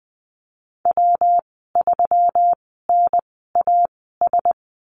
Morse code, heard as a keyed tone, about 20 wpm, spaced as for 15 wpm, 700 Hz. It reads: W3NAS
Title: 154 Lesson 25g - Letter p - Call Signs 20wpm